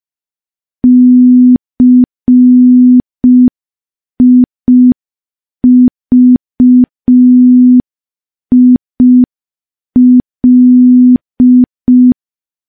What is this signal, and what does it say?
Morse code, heard as a keyed tone, about 5 wpm, 250 Hz. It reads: CIVIL
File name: morse / morse-training-05WPM